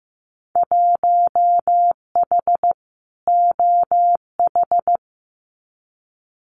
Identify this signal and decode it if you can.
Morse code, heard as a keyed tone, about 15 wpm, 700 Hz. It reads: 1H OH
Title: Winter Field Day Exchanges 15wpm